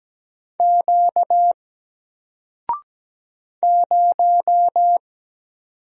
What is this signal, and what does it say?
Morse code, heard as a keyed tone, about 17 wpm, 700 Hz. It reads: Q 0
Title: Single Character (Letter:Number:Punctuation:Prosign) 17wpm